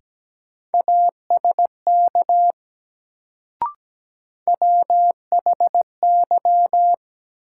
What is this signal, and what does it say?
Morse code, heard as a keyed tone, about 17 wpm, 700 Hz. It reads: ASK WHY
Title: Sentences from Top 200 Words - New Word Drill - 17wpm